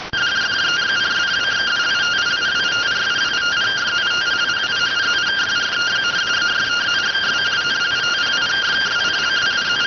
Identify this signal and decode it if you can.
Two-tone teleprinter signal, about 170 Hz shift.
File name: findings / SAGITAL nb203 fr61 3mm.gen-10 rx